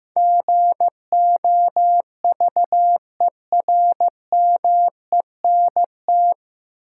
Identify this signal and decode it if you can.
Morse code, heard as a keyed tone, about 15 wpm, 700 Hz. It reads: GOVERMENT
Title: Sentences from Top 1300 Words - New Word Drill Alt - 15wpm